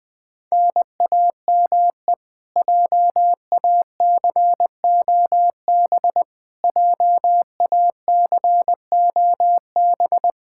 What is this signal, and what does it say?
Morse code, heard as a keyed tone, about 20 wpm, 700 Hz. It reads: NAME JACOB JACOB
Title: K1USN SST QSO Practice 20wpm